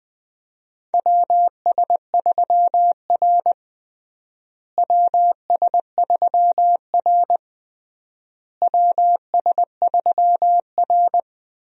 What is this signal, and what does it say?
Morse code, heard as a keyed tone, about 20 wpm, 700 Hz. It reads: WS3R WS3R WS3R
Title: USA Call Signs - 4 Characters - Repeated 3x - 20wpm